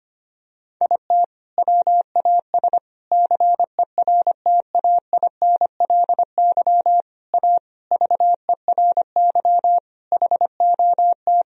Morse code, heard as a keyed tone, about 25 wpm, 700 Hz. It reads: IT WAS CERTAINLY A VERY HOT
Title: Sentences from Top 1000 Words - Review Part 3 - 25wpm